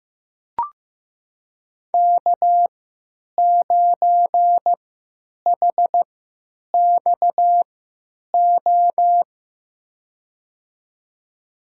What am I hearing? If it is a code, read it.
Morse code, heard as a keyed tone, about 15 wpm, spaced as for 9 wpm, 700 Hz. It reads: K9HXO
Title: USA Call Signs - 5 Characters - 3x Character Spacing - 15wpm